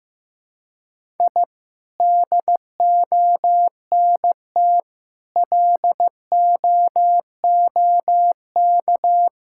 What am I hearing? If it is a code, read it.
Morse code, heard as a keyed tone, about 15 wpm, 700 Hz. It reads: I DONT LOOK
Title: Sentences from Top 1000 Words - Review Part 5 - 15wpm